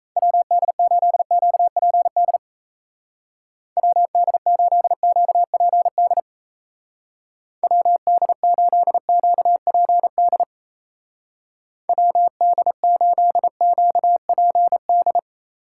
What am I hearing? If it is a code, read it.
Morse code, heard as a keyed tone, about 35 wpm, 700 Hz. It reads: WD8QPD WD8QPD WD8QPD WD8QPD